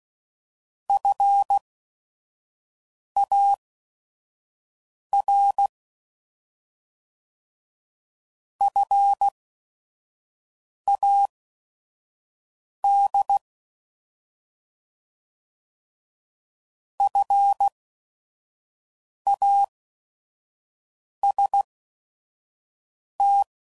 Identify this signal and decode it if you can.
Morse code, heard as a keyed tone, about 16 wpm, spaced as for 5 wpm, 800 Hz. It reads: FAR FAD FAST
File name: Prefixes / By First Letter / Fs